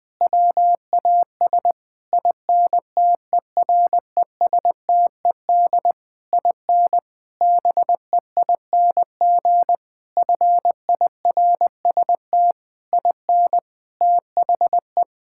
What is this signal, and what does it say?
Morse code, heard as a keyed tone, about 20 wpm, 700 Hz. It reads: WAS INTERESTED IN BEING FIRST IN THE
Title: Sentences from Top 1600 Words 20wpm